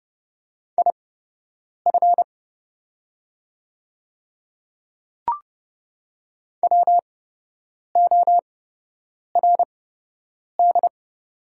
Morse code, heard as a keyed tone, about 30 wpm, spaced as for 8 wpm, 700 Hz. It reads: IF WORD
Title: 8x Character Spacing - Top 200 Words - 30wpm